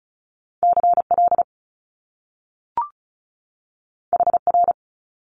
Morse code, heard as a keyed tone, about 35 wpm, 700 Hz. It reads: C L H R
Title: Top Two Letter Combinations in Words 35wpm